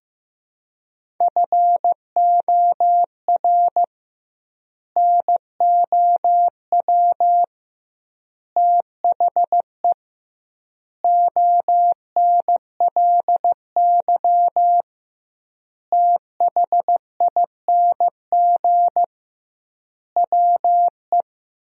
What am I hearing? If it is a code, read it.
Morse code, heard as a keyed tone, about 15 wpm, 700 Hz. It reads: FOR NOW THE ONLY THING WE
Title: 2x Word-Spacing - Sentences from Top 200 Words - 15wpm